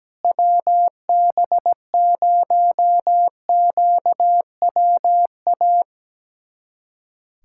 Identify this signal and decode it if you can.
Morse code, heard as a keyed tone, about 17 wpm, 700 Hz. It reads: WB0QWA
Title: USA Callsign DE Callsign 17wpm